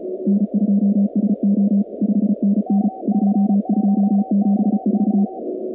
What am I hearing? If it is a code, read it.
Morse code, heard as a keyed tone, about 34 wpm, 200 Hz. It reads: N2SO 5NN 2274
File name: contests / 2019-cw-wpx-kt5j-k5pi- -2330 0-46